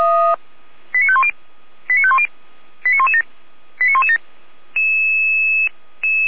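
Five-tone selective call (ZVEI), heard as quick runs of five tones, tones of about 70 ms. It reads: 89510 89510 89117 89117
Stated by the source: ZVEI five-tone call 89104 received on SDR, Feuerwehr Orlamünde.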